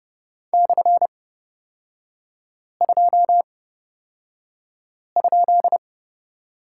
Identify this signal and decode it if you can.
Morse code, heard as a keyed tone, about 30 wpm, spaced as for 5 wpm, 700 Hz. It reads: /2?
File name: Lesson 36e - BT - Three Characters 30wpm